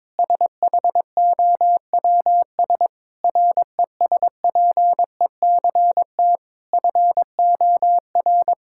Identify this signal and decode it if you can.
Morse code, heard as a keyed tone, about 22 wpm, 700 Hz. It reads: SHOWS RESPECT FOR
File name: Difficult Sentences with Hints - 22wpm